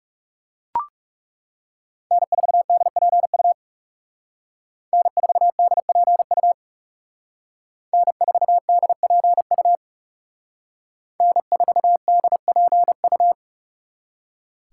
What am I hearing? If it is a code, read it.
Morse code, heard as a keyed tone, about 38 wpm, 700 Hz. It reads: N4DPU N4DPU N4DPU N4DPU